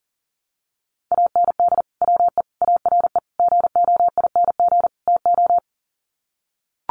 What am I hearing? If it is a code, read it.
Morse code, heard as a keyed tone, about 40 wpm, 700 Hz. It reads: AND WE ARE GOING TO
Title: Sets of 5 Words Encore2 40wpm